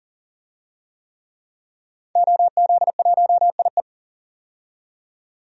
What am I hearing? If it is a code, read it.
Morse code, heard as a keyed tone, about 40 wpm, 700 Hz. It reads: OZ1IE
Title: DXCC-100 Call Signs - 5 Characters - 40wpm